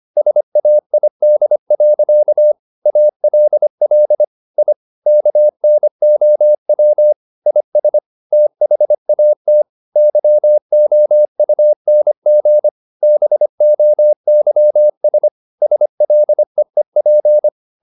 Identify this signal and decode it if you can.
Morse code, heard as a keyed tone, about 25 wpm, 600 Hz. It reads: SAID. ALL I KNOW IS THAT YOUNG BOYS SLEEP
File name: the old man and the sea part 1 25wpm